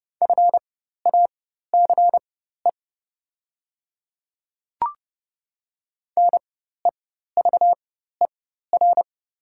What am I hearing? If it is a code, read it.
Morse code, heard as a keyed tone, about 30 wpm, spaced as for 14 wpm, 700 Hz. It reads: FACE NEVER
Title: 4x Character Spacing - Top 300 Words - 30wpm